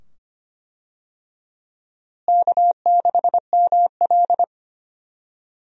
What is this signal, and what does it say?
Morse code, heard as a keyed tone, about 25 wpm, 700 Hz. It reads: K6ML
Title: USA Call Signs - 4 Characters - Repeated 2x - 25wpm